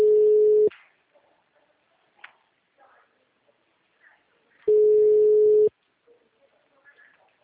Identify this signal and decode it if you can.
Ringback tone (Europe, 425 Hz), with speech.